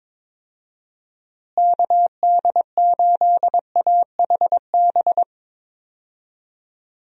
Morse code, heard as a keyed tone, about 21 wpm, 700 Hz. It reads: KD8AHB
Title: USA Call Signs - 6 Characters - Repeated 2x - 22wpm